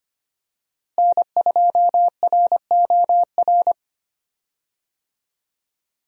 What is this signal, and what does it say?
Morse code, heard as a keyed tone, about 25 wpm, 700 Hz. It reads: N2ROR
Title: USA Call Signs - 5 Characters - Repeated 2x - 25wpm